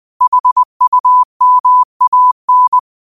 Morse code, heard as a keyed tone, about 20 wpm, 1000 Hz. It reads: HUMAN